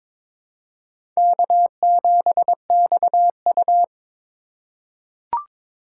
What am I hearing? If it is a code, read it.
Morse code, heard as a keyed tone, about 22 wpm, 700 Hz. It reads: K7XU
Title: USA Call Signs - 2x Character Spacing - 22wpm